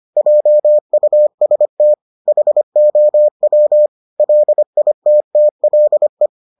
Morse code, heard as a keyed tone, about 25 wpm, 600 Hz. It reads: JUST HOW LITTLE